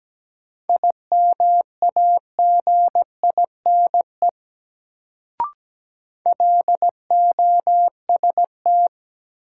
Morse code, heard as a keyed tone, about 17 wpm, 700 Hz. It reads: IMAGINE LOST
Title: Sentences from Top 1400 Words - New Word Drill Alt - 17wpm